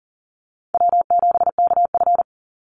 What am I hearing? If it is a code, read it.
Morse code, heard as a keyed tone, about 40 wpm, 700 Hz. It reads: W7KF
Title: Easy Call Signs Repeated 3x - 40wpm